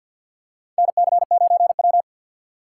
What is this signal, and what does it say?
Morse code, heard as a keyed tone, about 46 wpm, 700 Hz. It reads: NC9W
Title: K1USN SST QSO Practice - 50wpm